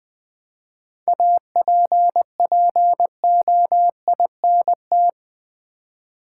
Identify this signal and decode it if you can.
Morse code, heard as a keyed tone, about 20 wpm, 700 Hz. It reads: APPOINT
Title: 7 Letter Words - First 3 Letters Given - 20wpm